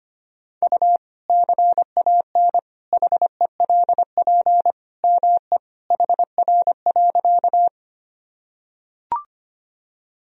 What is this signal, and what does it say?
Morse code, heard as a keyed tone, about 25 wpm, 700 Hz. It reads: U CAN HELP ME HR.
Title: Sentences from Top 200 Words and Abbreviations 25wpm